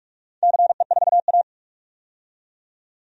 Morse code, heard as a keyed tone, about 44 wpm, 700 Hz. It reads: CE4A
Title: DXCC-100 Call Signs - 4 Characters - 45wpm